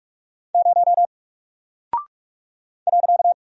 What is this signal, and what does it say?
Morse code, heard as a keyed tone, about 45 wpm, 700 Hz. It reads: 0 .
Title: Single Character (Letter:Number:Punctuation:Prosign) Alt 45wpm